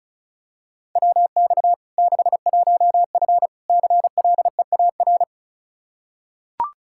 Morse code, heard as a keyed tone, about 36 wpm, 700 Hz. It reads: WX 61F CLEAR
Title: K1USN SST QSO Practice 35wpm